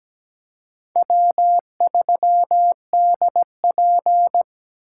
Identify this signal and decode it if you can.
Morse code, heard as a keyed tone, about 20 wpm, 700 Hz. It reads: W3DP
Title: Speed Racing - USA 1x2 Call Signs 17wpm